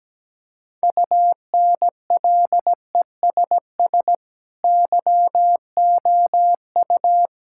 Morse code, heard as a keyed tone, about 17 wpm, 700 Hz. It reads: UNLESS YOU
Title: Sentences from Top 1000 Words - Review Part 6 - 17wpm